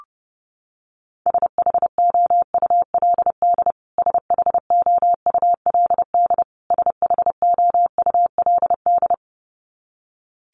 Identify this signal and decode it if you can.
Morse code, heard as a keyed tone, about 30 wpm, 700 Hz. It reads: SHOULD SHOULD SHOULD
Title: Top 500 Words Repeated 30wpm